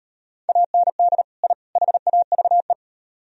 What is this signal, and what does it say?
Morse code, heard as a keyed tone, about 37 wpm, 700 Hz. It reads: AND I HAVE